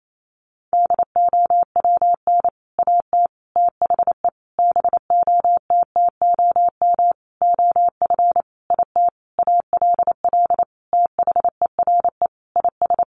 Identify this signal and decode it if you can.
Morse code, heard as a keyed tone, about 28 wpm, 700 Hz. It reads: DOWN AT THE BOTTOM OF IT ALL THERE IS
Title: Sentences from Top 1100 Words 28wpm